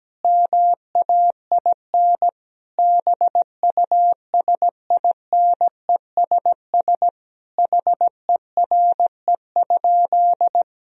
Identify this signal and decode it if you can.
Morse code, heard as a keyed tone, about 17 wpm, 700 Hz. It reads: MAIN BUSINESS HERE?
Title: Sentences from Top 1000 Words - Review Part 2 - 17wpm